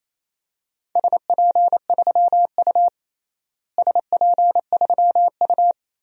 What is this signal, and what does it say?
Morse code, heard as a keyed tone, about 28 wpm, 700 Hz. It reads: SP3U SP3U
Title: DXCC-100 Call Signs - 4 Characters - Repeated 2x - 28wpm